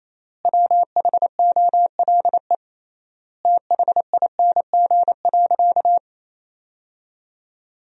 Morse code, heard as a keyed tone, about 28 wpm, 700 Hz. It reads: WHOLE THING.
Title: 3x Word-Spacing - Sentences from Top 400 Words - 28wpm